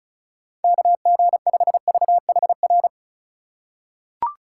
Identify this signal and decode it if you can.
Morse code, heard as a keyed tone, about 36 wpm, 700 Hz. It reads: KG5VHR
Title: USA Call Signs - 6 Characters - 4x Character Spacing - 35wpm